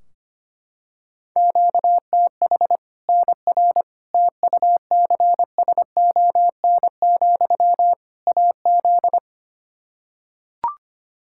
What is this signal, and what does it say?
Morse code, heard as a keyed tone, about 25 wpm, 700 Hz. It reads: QTH NR TUCSON, AZ
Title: K1USN SST QSO Practice 25wpm